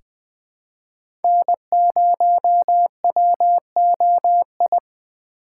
Morse code, heard as a keyed tone, about 20 wpm, 700 Hz. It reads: N0WOI
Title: K1USN SST QSO Practice - 20wpm